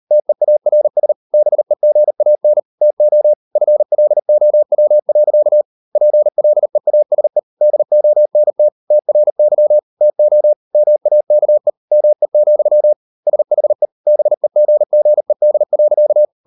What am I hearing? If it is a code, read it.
Morse code, heard as a keyed tone, about 38 wpm, 600 Hz. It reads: TEARS BEGAN TO FLOW. PLEASE DONT TRY TO MAKE ME, SHE BEGGED.